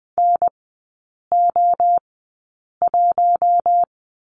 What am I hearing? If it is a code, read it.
Morse code, heard as a keyed tone, about 20 wpm, spaced as for 8 wpm, 700 Hz. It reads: NO1
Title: 060 Lesson 11e - Letter h - Three Characters 20wpm